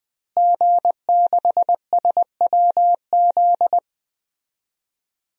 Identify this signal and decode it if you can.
Morse code, heard as a keyed tone, about 20 wpm, 700 Hz. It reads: G6SWZ
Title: DXCC-100 Call Signs - 5 Characters - Repeated 2x - 20wpm